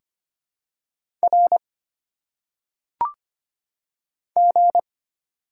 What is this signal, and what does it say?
Morse code, heard as a keyed tone, about 25 wpm, 700 Hz. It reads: R G
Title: Single Letters - Phonetic Pronunciation 25wpm